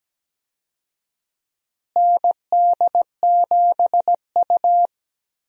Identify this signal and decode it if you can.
Morse code, heard as a keyed tone, about 17 wpm, 700 Hz. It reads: ND7U
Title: USA Call Signs - 4 Characters - Repeated 3x - 17wpm